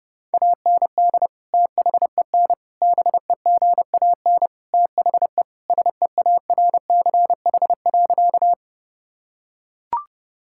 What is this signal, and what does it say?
Morse code, heard as a keyed tone, about 30 wpm, 700 Hz. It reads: AND THEN BEGAN THE SEARCH.
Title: Sentences from Top 1000 Words - Review Part 5 - 30wpm